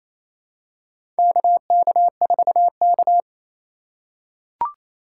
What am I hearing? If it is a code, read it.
Morse code, heard as a keyed tone, about 28 wpm, 700 Hz. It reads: KK4K
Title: USA Call Signs - 4 Characters - 8x Character Spacing - 28wpm